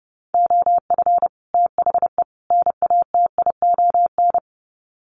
Morse code, heard as a keyed tone, about 30 wpm, 700 Hz. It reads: OF THE NATION
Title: Sets of 3 Words 30wpm